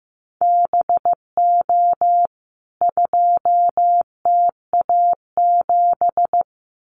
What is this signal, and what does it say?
Morse code, heard as a keyed tone, about 15 wpm, 700 Hz. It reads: BO 2TA7
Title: CWT Exchange 15wpm